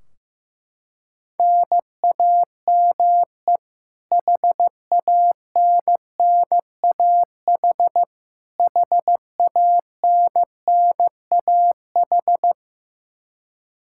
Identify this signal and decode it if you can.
Morse code, heard as a keyed tone, about 15 wpm, 700 Hz. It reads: NAME HANNAH HANNAH
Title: Practice QSO - Exchange 1 - Beginning - 15wpm